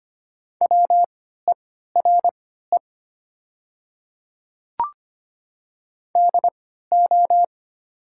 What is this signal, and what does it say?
Morse code, heard as a keyed tone, about 25 wpm, spaced as for 14 wpm, 700 Hz. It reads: WERE DO
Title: 3x Character Spacing - Top 200 Words - 25wpm